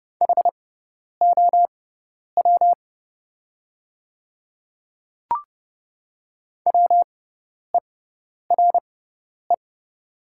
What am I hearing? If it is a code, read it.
Morse code, heard as a keyed tone, about 30 wpm, spaced as for 10 wpm, 700 Hz. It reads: HOW WERE
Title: 6x Character Spacing - Top 300 Words - 30wpm